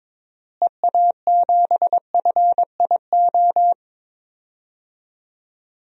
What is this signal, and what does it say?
Morse code, heard as a keyed tone, about 22 wpm, 700 Hz. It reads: EA7FIO
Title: DXCC-100 Call Signs - 6 Characters - Repeated 2x - 22wpm